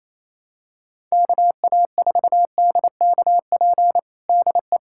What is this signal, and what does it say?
Morse code, heard as a keyed tone, about 28 wpm, 700 Hz. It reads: KA4DKP DE
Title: USA Callsign DE Callsign 28wpm